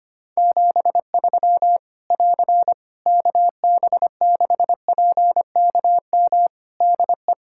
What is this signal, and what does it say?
Morse code, heard as a keyed tone, about 25 wpm, 700 Hz. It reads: 73 <AR> KB6PKM DE
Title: Practice QSO - Exchange 4 - Ending - 25wpm